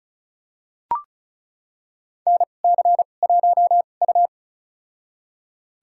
Morse code, heard as a keyed tone, about 36 wpm, spaced as for 25 wpm, 700 Hz. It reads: NC1U
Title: USA Call Signs - 4 Characters - 2x Character Spacing - 35wpm